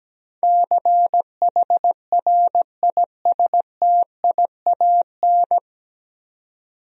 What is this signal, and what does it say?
Morse code, heard as a keyed tone, about 17 wpm, 700 Hz. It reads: CHRISTIAN
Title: USA Common Names - 17wpm